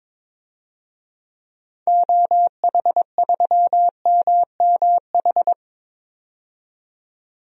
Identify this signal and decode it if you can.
Morse code, heard as a keyed tone, about 22 wpm, 700 Hz. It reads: OH3MMH
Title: DXCC-100 Call Signs - 6 Characters - 22wpm